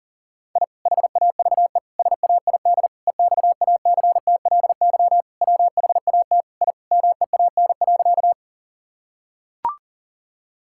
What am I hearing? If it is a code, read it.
Morse code, heard as a keyed tone, about 40 wpm, 700 Hz. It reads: I HAVE SAID EXACTLY WHAT I MEAN.
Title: Sentences from Top 1000 Words - Review Part 6 - 40wpm